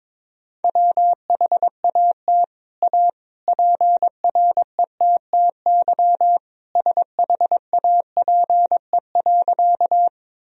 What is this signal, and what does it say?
Morse code, heard as a keyed tone, about 21 wpm, 700 Hz. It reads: WHAT A PRETTY SHAPE.
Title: Sentences from Top 1000 Words - Review Part 3 - 22wpm